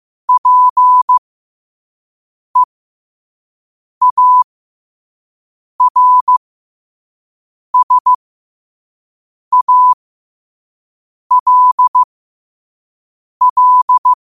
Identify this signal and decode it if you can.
Morse code, heard as a keyed tone, about 15 wpm, spaced as for 5 wpm, 1000 Hz. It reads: PEARSALL
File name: qotdfiles / 5wpm